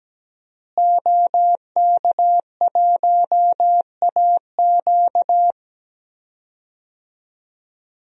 Morse code, heard as a keyed tone, about 17 wpm, 700 Hz. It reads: OK1AQ
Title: DXCC-100 Call Signs - 5 Characters - Repeated 2x - 17wpm